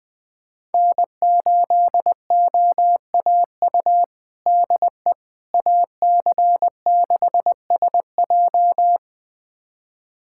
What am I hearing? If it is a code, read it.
Morse code, heard as a keyed tone, about 21 wpm, 700 Hz. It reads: N8OAU DE AC6SJ
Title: USA Callsign DE Callsign - 5 Characters - 20wpm